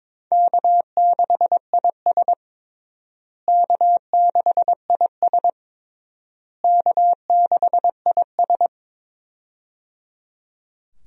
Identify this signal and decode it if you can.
Morse code, heard as a keyed tone, about 22 wpm, 700 Hz. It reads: K6IS K6IS K6IS
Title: USA Call Signs - 4 Characters - Repeated 3x - 22wpm